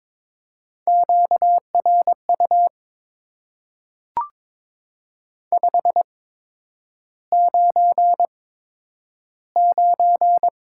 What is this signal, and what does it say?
Morse code, heard as a keyed tone, about 22 wpm, spaced as for 9 wpm, 700 Hz. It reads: QRU 599
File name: Top 90 QSO Elements - 8x Character Spacing - 22wpm